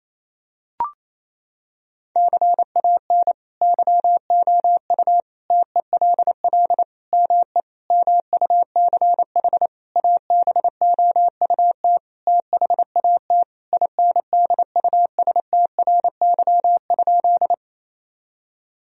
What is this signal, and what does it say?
Morse code, heard as a keyed tone, about 28 wpm, 700 Hz. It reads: CAN YOU TELL ME MUCH ABOUT THAT INDUSTRY?
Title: Sentences from Top 1000 Words - Review Part 4 - 28wpm